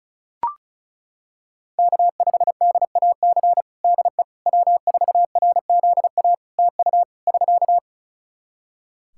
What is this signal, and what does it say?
Morse code, heard as a keyed tone, about 36 wpm, 700 Hz. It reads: K5DAC DE W4RZA TU <SK>
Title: Practice QSO - Exchange 4 - Ending - 35wpm